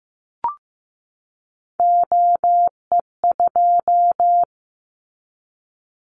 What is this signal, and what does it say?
Morse code, heard as a keyed tone, about 15 wpm, 700 Hz. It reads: OE2
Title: Callsign Prefixes for Top 100 DXCC - 15wpm